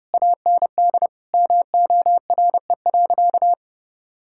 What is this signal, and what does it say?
Morse code, heard as a keyed tone, about 30 wpm, 700 Hz. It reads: AND MORE.
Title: Sentences from Top 1000 Words - Review Part 5 - 30wpm